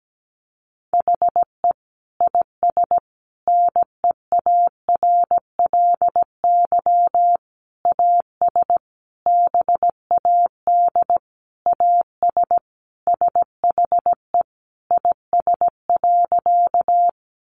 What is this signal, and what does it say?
Morse code, heard as a keyed tone, about 17 wpm, 700 Hz. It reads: HE IS NEARLY AS BAD AS SHE IS.
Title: Sentences from Top 1300 Words 17wpm